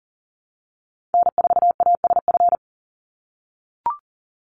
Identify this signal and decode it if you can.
Morse code, heard as a keyed tone, about 40 wpm, 700 Hz. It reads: N4ASF
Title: Call Signs - 40wpm